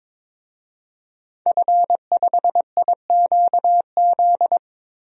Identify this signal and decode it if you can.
Morse code, heard as a keyed tone, about 22 wpm, 700 Hz. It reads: F5IQZ
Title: DXCC-100 Call Signs - 5 Characters - Repeated 2x - 22wpm